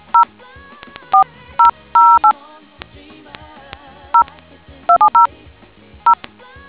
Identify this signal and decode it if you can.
DTMF, heard as telephone touch tones, tones of about 110 ms. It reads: 0 4008 0270 0